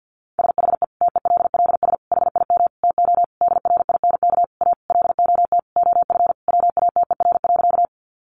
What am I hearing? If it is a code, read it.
Morse code, heard as a keyed tone, about 50 wpm, 700 Hz. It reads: SHE TELLS HIM TO DRINK A LOT OF WATER.
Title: Additional Easy Phrases 50wpm